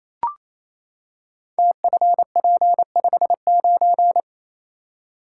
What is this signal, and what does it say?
Morse code, heard as a keyed tone, about 29 wpm, 700 Hz. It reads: TFP59
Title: Code Groups - 5 Letter:Numbers - 28wpm